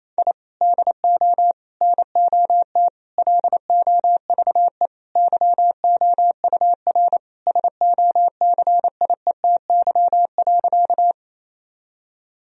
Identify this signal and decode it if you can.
Morse code, heard as a keyed tone, about 28 wpm, 700 Hz. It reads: I DO NOT LOVE YOUR SOCIETY.
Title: Sentences from Top 1000 Words - Review Part 4 - 28wpm